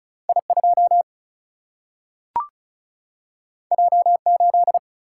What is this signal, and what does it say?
Morse code, heard as a keyed tone, about 36 wpm, 700 Hz. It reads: I2 J8
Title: Single-Letter Single-Number 35wpm